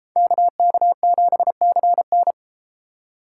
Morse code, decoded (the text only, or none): KK7CN